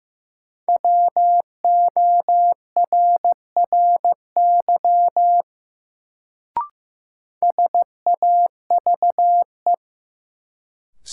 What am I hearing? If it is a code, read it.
Morse code, heard as a keyed tone, about 15 wpm, 700 Hz. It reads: WORRY SAVE
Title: Sentences from Top 1100 Words - New Word Drill Alt - 15wpm